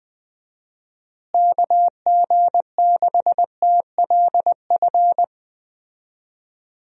Morse code, heard as a keyed tone, about 21 wpm, 700 Hz. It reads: KG6TLF
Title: USA Call Signs - 6 Characters - Repeated 3x - 20wpm